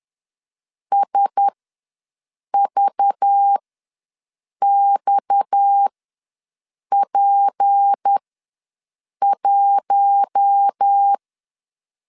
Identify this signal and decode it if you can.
Morse code, heard as a keyed tone, about 10 wpm, spaced as for 6 wpm, 800 Hz. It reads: SVXP1
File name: LEZIONI / 6 PKX0 / 6 PKX ws 10wpm 3706